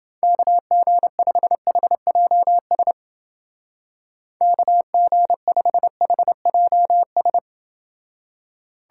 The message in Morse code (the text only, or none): KG5HJS KG5HJS